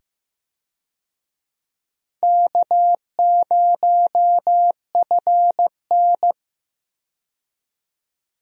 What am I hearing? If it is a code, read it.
Morse code, heard as a keyed tone, about 15 wpm, 700 Hz. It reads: K0FN
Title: USA Call Signs - 4 Characters - 15wpm